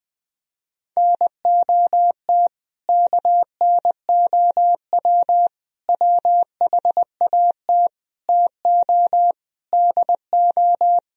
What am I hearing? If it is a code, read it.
Morse code, heard as a keyed tone, about 20 wpm, 700 Hz. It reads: NOT KNOW WHAT TO DO
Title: Speed-Racing - Sets of 5 Words 17wpm